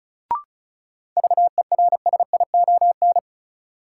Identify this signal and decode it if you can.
Morse code, heard as a keyed tone, about 34 wpm, 700 Hz. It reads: VERSION
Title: Sentences from Top 1400 Words - New Word Drill Alt - 35wpm